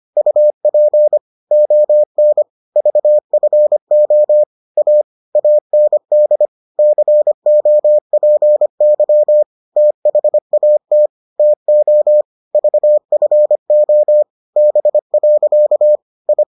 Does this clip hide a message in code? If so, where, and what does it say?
Yes: Morse code from 0.2 to 16.4 s, heard as a keyed tone, about 25 wpm, 600 Hz. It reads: UP ON VFO A AND COPY THAT TO VFO B. I